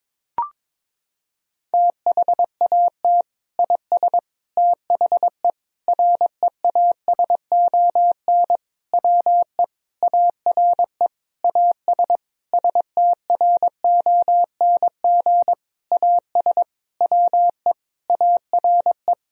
Morse code, heard as a keyed tone, about 22 wpm, 700 Hz. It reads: THAT IS THE REASON WE ARE AS STRONG AS WE ARE
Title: Sentences from Top 1000 Words - Review Part 4 - 22wpm